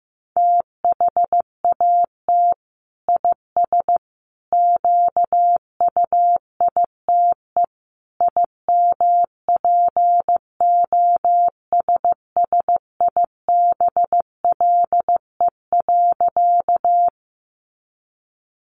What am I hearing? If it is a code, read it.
Morse code, heard as a keyed tone, about 15 wpm, 700 Hz. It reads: THAT IS QUITE IMPOSSIBLE.
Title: Sentences from Top 1100 Words 15wpm